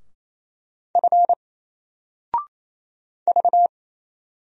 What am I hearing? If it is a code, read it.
Morse code, heard as a keyed tone, about 27 wpm, 700 Hz. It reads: F V
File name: Single-Letter-Number-28wpm